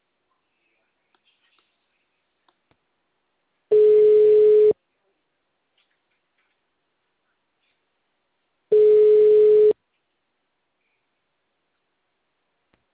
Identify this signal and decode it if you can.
Ringback tone (Europe, 425 Hz).
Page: Forca - RJ